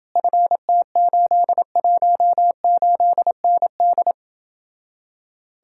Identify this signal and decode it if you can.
Morse code, heard as a keyed tone, about 29 wpm, 700 Hz. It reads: FT818ND